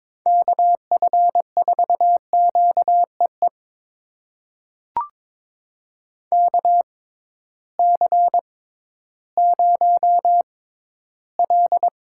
Morse code, heard as a keyed tone, about 22 wpm, spaced as for 13 wpm, 700 Hz. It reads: KF4QEE KC0L
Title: USA Call Signs - 6 Characters - 6x Character Spacing - 22wpm